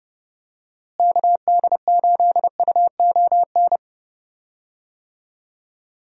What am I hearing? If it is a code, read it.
Morse code, heard as a keyed tone, about 32 wpm, 700 Hz. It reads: KD8UON
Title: USA Call Signs - 6 Characters - Repeated 2x - 30wpm